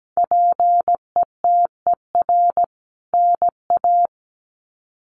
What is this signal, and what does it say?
Morse code, heard as a keyed tone, about 17 wpm, 700 Hz. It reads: PETER NA
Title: CWT Exchange 17wpm